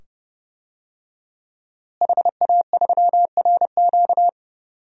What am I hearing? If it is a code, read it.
Morse code, heard as a keyed tone, about 30 wpm, 700 Hz. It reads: HA3RQ
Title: DXCC-100 Call Signs - 5 Characters - Repeated 2x - 30wpm